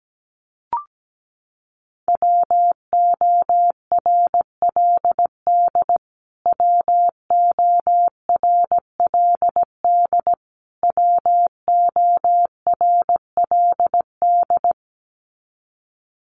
Morse code, heard as a keyed tone, about 17 wpm, 700 Hz. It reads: WORLD WORLD WORLD
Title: Top 500 Words Repeated 17wpm